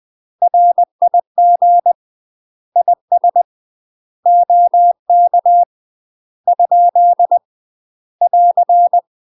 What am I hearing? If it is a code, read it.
Morse code, heard as a keyed tone, about 20 wpm, 700 Hz. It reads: RIG IS OK ? <AR>
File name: UK MASTER QSOs TRAINING / 20WPM / UK MASTER QSOs 0004